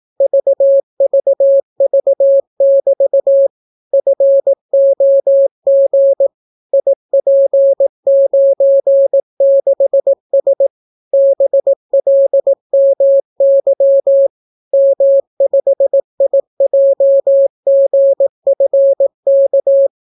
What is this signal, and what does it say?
Morse code, heard as a keyed tone, about 18 wpm, 550 Hz. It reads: VVV<BT> FOG IP96S BLMY M5IJGFK